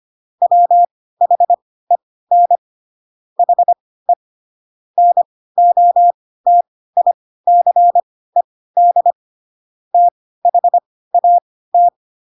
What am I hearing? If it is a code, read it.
Morse code, heard as a keyed tone, about 25 wpm, spaced as for 16 wpm, 700 Hz. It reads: WHEN HE NOTICED THAT
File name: Pinocchio / Pinocchio ch3 10 25 ik1hge